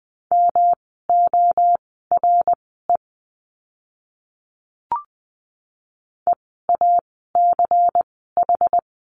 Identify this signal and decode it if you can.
Morse code, heard as a keyed tone, about 20 wpm, spaced as for 15 wpm, 700 Hz. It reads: MORE EACH
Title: 192 Lesson 31f - Letter v - Words 20wpm